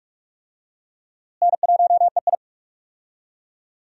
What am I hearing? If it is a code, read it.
Morse code, heard as a keyed tone, about 44 wpm, 700 Hz. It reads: N1EI
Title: USA Call Signs - 4 Characters - Repeated 2x - 45wpm